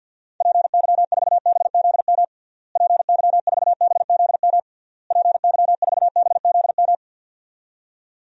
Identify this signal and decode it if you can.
Morse code, heard as a keyed tone, about 51 wpm, 700 Hz. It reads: PY4BZM PY4BZM PY4BZM
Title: DXCC-100 Call Signs - 6 Characters - Repeated 3x - 50wpm